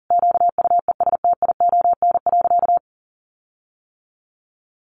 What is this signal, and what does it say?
Morse code, heard as a keyed tone, about 40 wpm, 700 Hz. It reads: QUESTION.
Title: Sentences from Top 1400 Words 40wpm